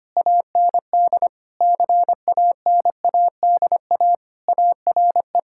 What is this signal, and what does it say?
Morse code, heard as a keyed tone, about 25 wpm, 700 Hz. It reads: AND CANADA ARE
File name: practice / 25wpm / Sentences with 3-Word Phrases - 25wpm